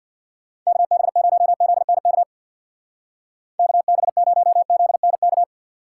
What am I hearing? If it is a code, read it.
Morse code, heard as a keyed tone, about 60 wpm, 700 Hz. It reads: KB0ZNK KB0ZNK
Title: Speed Racing - USA 2x3 Call Signs 50wpm